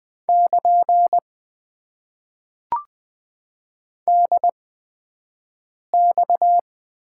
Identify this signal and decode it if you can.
Morse code, heard as a keyed tone, about 20 wpm, spaced as for 5 wpm, 700 Hz. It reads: <KN> DX
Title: Top 90 QSO Elements - 8x Character Spacing - 20wpm